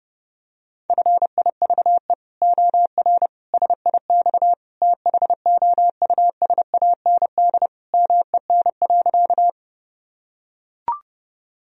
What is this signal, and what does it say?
Morse code, heard as a keyed tone, about 30 wpm, 700 Hz. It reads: FIVE OR SIX THOUSAND MEN.
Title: Sentences from Top 1000 Words - Review Part 3 - 30wpm